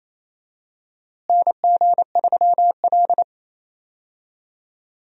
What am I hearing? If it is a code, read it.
Morse code, heard as a keyed tone, about 28 wpm, 700 Hz. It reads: NG3L
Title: USA Call Signs - 4 Characters - Repeated 2x - 28wpm